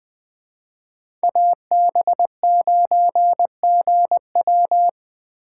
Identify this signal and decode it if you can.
Morse code, heard as a keyed tone, about 20 wpm, 700 Hz. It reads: AB9GW